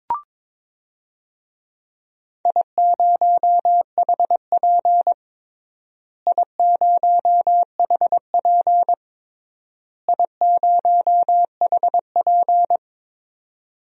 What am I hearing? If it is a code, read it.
Morse code, heard as a keyed tone, about 22 wpm, 700 Hz. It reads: I0HP I0HP I0HP
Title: DXCC-100 Call Signs - 4 Characters - Repeated 3x - 22wpm